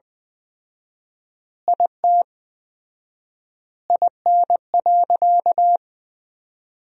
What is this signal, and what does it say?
Morse code, heard as a keyed tone, about 20 wpm, 700 Hz. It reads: IT IN.
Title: 4x Word-Spacing - Sentences from Top 200 Words - 20wpm